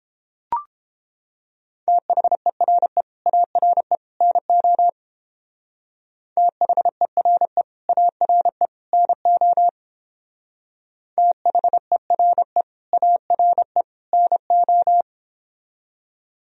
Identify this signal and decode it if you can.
Morse code, heard as a keyed tone, about 30 wpm, 700 Hz. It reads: THERE ARE NO THERE ARE NO THERE ARE NO